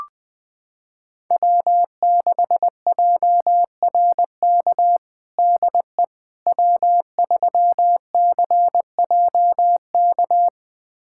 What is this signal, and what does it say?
Morse code, heard as a keyed tone, about 20 wpm, 700 Hz. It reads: W6JRK DE W3CJK
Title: USA Callsign DE Callsign - 5 Characters - 20wpm